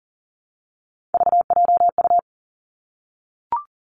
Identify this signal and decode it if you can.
Morse code, heard as a keyed tone, about 40 wpm, 700 Hz. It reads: VJU
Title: Three Challenging Letter-Number Combinations 40wpm